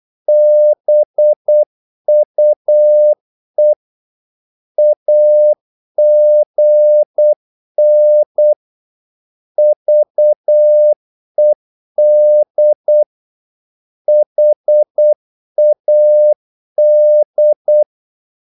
Morse code, heard as a keyed tone, about 8 wpm, 600 Hz. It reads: BUE AGN VED HAD
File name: Korte ord DK 8wpm